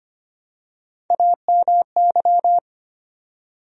Morse code, heard as a keyed tone, about 25 wpm, 700 Hz. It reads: AMY
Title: USA Common Names - Max 3 Letters - 25wpm